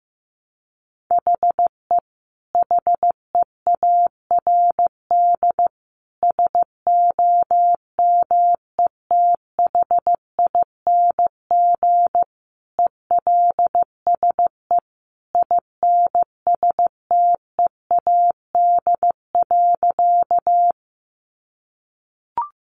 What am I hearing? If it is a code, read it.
Morse code, heard as a keyed tone, about 14 wpm, 700 Hz. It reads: HE HEARD SOMETHING ELSE INSTEAD.
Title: Sentences from Top 800 Words 15wpm